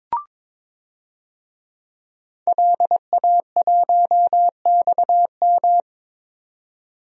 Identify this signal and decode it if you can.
Morse code, heard as a keyed tone, about 22 wpm, 700 Hz. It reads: LA1XM
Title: DXCC-100 Call Signs - 5 Characters - 22wpm